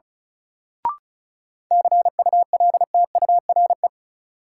Morse code, heard as a keyed tone, about 36 wpm, 700 Hz. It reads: CULTURE
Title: Sentences from Top 1000 Words - Review Part 6 - New Word Drill - 35wpm